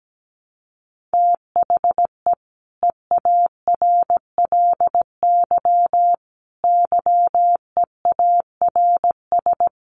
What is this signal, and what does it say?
Morse code, heard as a keyed tone, about 17 wpm, 700 Hz. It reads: THE EARLY YEARS
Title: Sets of 5 Words Encore 17wpm